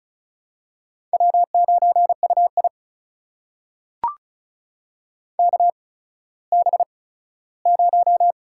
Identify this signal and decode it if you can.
Morse code, heard as a keyed tone, about 36 wpm, spaced as for 17 wpm, 700 Hz. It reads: W9UI KB0
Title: USA Call Signs - 8x Character Spacing - 35wpm